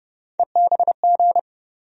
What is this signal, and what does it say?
Morse code, heard as a keyed tone, about 30 wpm, 700 Hz. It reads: EBG
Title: USA Call Sign Postfixes - 3 Letters 30wpm